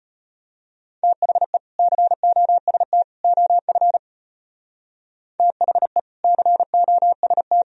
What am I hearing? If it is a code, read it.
Morse code, heard as a keyed tone, about 31 wpm, 700 Hz. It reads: THE COST OF THE COST